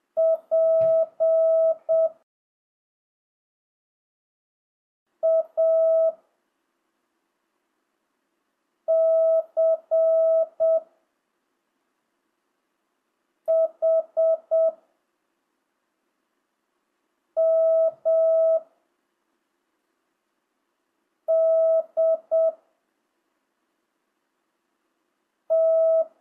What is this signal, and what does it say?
Morse code, heard as a keyed tone, about 7 wpm, 1300 Hz. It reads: P A C H M D T
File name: jh8cbh / 2025cw